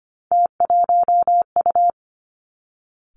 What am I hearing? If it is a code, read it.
Morse code, heard as a keyed tone, about 25 wpm, 700 Hz. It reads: T1U
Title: Three Challenging Letter-Number Combinations 25wpm